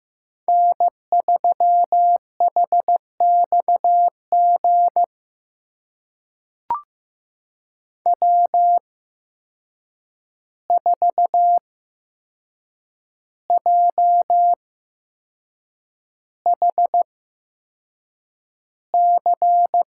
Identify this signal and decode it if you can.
Morse code, heard as a keyed tone, about 15 wpm, spaced as for 6 wpm, 700 Hz. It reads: N3HXG W4JHC
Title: USA Call Signs - 5 Characters - 8x Character Spacing - 15wpm